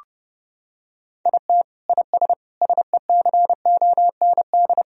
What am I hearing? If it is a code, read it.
Morse code, heard as a keyed tone, about 30 wpm, 700 Hz. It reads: IT IS SECOND
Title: Sentences from Top 1000 Words - Review Part 1 - 30wpm